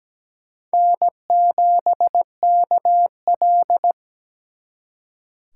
Morse code, heard as a keyed tone, about 20 wpm, 700 Hz. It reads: N7KL
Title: Speed Racing - USA 1x2 Call Signs 17wpm